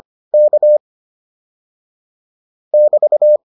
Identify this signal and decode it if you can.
Morse code, heard as a keyed tone, about 25 wpm, 600 Hz. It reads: K <BT>